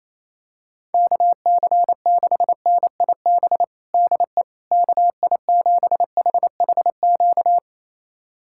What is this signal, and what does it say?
Morse code, heard as a keyed tone, about 28 wpm, 700 Hz. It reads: KC6NIB DE KI7HHQ
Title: USA Callsign DE Callsign - 6 Characters - 28wpm